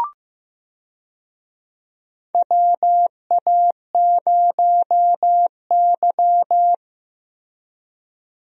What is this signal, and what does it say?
Morse code, heard as a keyed tone, about 15 wpm, 700 Hz. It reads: WA0Y
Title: USA Call Signs - 4 Characters - Repeated 3x - 15wpm